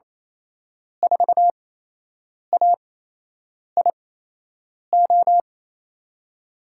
Morse code, heard as a keyed tone, about 28 wpm, spaced as for 8 wpm, 700 Hz. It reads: 4AIO
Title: USA Call Signs - 6 Characters - 8x Character Spacing - 28wpm